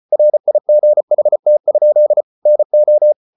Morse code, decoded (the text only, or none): RIGHT? NO